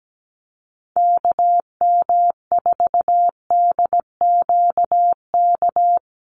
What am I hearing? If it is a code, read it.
Morse code, heard as a keyed tone, about 17 wpm, 700 Hz. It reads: KM4DQK
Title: USA Call Signs Repeated 2x 17wpm